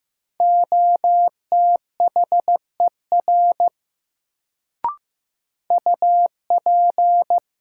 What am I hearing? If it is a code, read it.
Morse code, heard as a keyed tone, about 15 wpm, 700 Hz. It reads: OTHER UP
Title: Sentences from Top 100 Words - New Word Drill - 15wpm